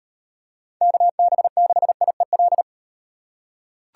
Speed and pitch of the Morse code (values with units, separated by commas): 35 wpm, 700 Hz